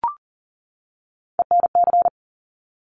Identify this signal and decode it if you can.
Morse code, heard as a keyed tone, about 38 wpm, 700 Hz. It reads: ENC
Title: Top Three Letter Combinations in Words 40wpm